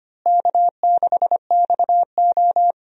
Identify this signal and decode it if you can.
Morse code, heard as a keyed tone, about 25 wpm, 700 Hz. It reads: K6XO
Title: USA Call Signs - 4 Characters - Repeated 3x - 25wpm